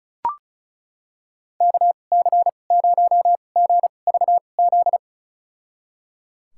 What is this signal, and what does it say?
Morse code, heard as a keyed tone, about 36 wpm, spaced as for 25 wpm, 700 Hz. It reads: KC0GVZ
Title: USA Call Signs - 6 Characters - 2x Character Spacing - 35wpm